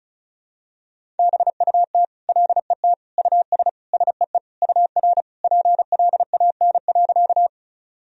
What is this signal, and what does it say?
Morse code, heard as a keyed tone, about 36 wpm, 700 Hz. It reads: BUT LET US SEE UR PLAN.
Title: Sentences from Top 500 Words and Abbreviations 35wpm